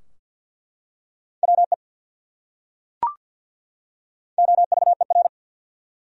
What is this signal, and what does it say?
Morse code, heard as a keyed tone, about 49 wpm, 700 Hz. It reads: WE OVER
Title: Top 100 Words 50wpm